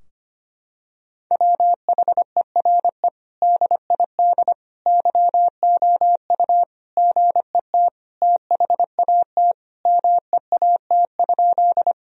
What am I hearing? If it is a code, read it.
Morse code, heard as a keyed tone, about 25 wpm, 700 Hz. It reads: WHERE DID YOU GET THAT MEAT?
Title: Sentences from Top 1000 Words - Review Part 3 - 25wpm